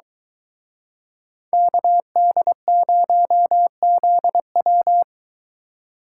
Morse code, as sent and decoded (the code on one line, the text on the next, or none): -.- -.. ----- --.. .--
KD0ZW